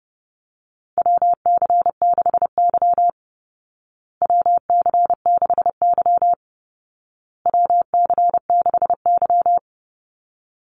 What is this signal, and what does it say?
Morse code, heard as a keyed tone, about 30 wpm, 700 Hz. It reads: WC6Y WC6Y WC6Y
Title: Call Signs Repeated 2x - 30wpm